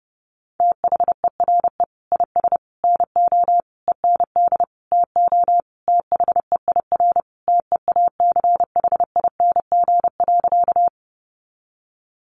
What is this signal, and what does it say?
Morse code, heard as a keyed tone, about 28 wpm, 700 Hz. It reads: THERE IS NO END TO THEIR TEACHING.
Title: Sentences from Top 1100 Words 30wpm